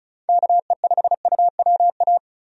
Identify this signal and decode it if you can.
Morse code, heard as a keyed tone, about 34 wpm, 700 Hz. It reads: KE5UWA
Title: USA Call Signs - 6 Characters - 35wpm